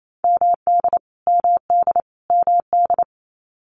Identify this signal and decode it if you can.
Morse code, heard as a keyed tone, about 28 wpm, 700 Hz. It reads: MD MD MD
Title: US States Repeated 3x 28wpm